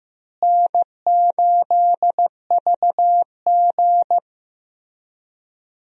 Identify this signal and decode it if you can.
Morse code, heard as a keyed tone, about 15 wpm, 700 Hz. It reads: N8VG
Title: USA Call Signs - 4 Characters - Repeated 2x - 15wpm